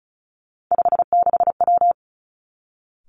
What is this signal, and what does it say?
Morse code, heard as a keyed tone, about 36 wpm, 700 Hz. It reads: 56W
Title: Three Challenging Letter-Number Combinations 35wpm